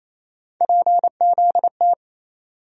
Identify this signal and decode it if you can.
Morse code, heard as a keyed tone, about 28 wpm, 700 Hz. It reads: PZT
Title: USA Call Sign Postfixes - 3 Letters 28wpm